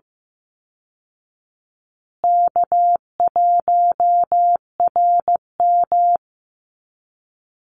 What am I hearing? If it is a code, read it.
Morse code, heard as a keyed tone, about 15 wpm, 700 Hz. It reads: K1RM
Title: Easy Call Signs Repeated 2x - 15wpm